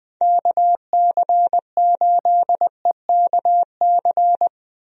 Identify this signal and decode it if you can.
Morse code, heard as a keyed tone, about 20 wpm, 700 Hz. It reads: KC8EKC
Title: USA Call Signs - 6 Characters - Repeated 2x - 20wpm